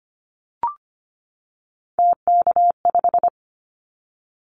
Morse code, heard as a keyed tone, about 25 wpm, 700 Hz. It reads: TK5
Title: Callsign Prefixes for Top 100 DXCC - 25wpm